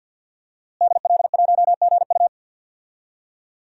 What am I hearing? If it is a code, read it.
Morse code, heard as a keyed tone, about 50 wpm, 700 Hz. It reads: DL1GU
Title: DXCC-100 Call Signs - 5 Characters - Repeated 2x - 50wpm